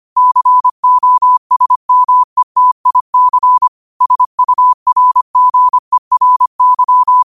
Morse code, heard as a keyed tone, about 25 wpm, 1000 Hz. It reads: COSMETIC SURGERY